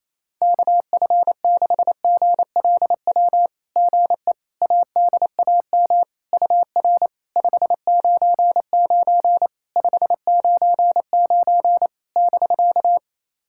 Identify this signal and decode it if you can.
Morse code, heard as a keyed tone, about 28 wpm, 700 Hz. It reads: KF6GLW GE ADAM UR 599 599 <BK>
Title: SOTA QSO Practice 28wpm